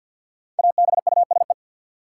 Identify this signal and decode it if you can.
Morse code, heard as a keyed tone, about 47 wpm, 700 Hz. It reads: ABUSE
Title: Sentences from Top 1300 Words - New Word Drill - 50wpm